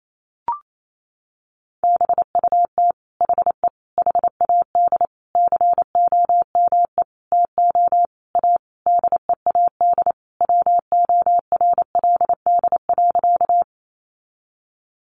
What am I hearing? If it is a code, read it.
Morse code, heard as a keyed tone, about 28 wpm, 700 Hz. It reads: BUT HE HAD COME TO A DEAD WORLD.
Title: Sentences from Top 1500 Words 28wpm